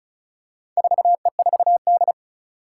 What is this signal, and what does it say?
Morse code, heard as a keyed tone, about 35 wpm, 700 Hz. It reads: 4E4D
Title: Four Challenging Letter-Number Combinations 35wpm